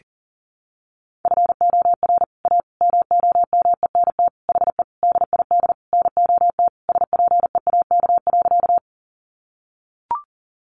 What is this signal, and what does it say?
Morse code, heard as a keyed tone, about 40 wpm, 700 Hz. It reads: FOR A MOMENT HE DID NOT SPEAK.
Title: Sentences from Top 1100 Words 40wpm